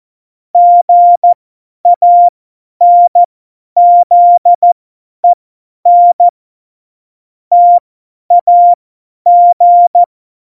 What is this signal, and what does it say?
Morse code, heard as a keyed tone, about 14 wpm, spaced as for 10 wpm, 700 Hz. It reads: GANZEN TAG
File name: Morsekurs / Witze / Witz 16. April 2019, 50 F70